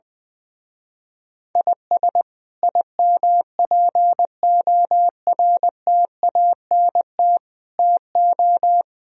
Morse code, heard as a keyed tone, about 19 wpm, 700 Hz. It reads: IS IMPORTANT TO
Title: Speed-Racing - Sets of 5 Words 20wpm